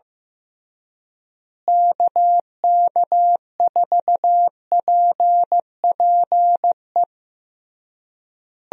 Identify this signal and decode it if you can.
Morse code, heard as a keyed tone, about 15 wpm, 700 Hz. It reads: KK4PPE
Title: USA Call Signs - 6 Characters - Repeated 3x - 15wpm